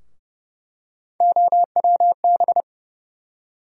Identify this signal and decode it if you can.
Morse code, heard as a keyed tone, about 30 wpm, 700 Hz. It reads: OWB
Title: USA Call Sign Postfixes - 3 Letters 30wpm